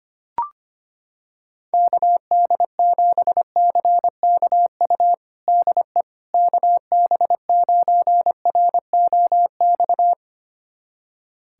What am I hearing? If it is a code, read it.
Morse code, heard as a keyed tone, about 25 wpm, 700 Hz. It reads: KD7CKU DE KB9ROX
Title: USA Callsign DE Callsign 25wpm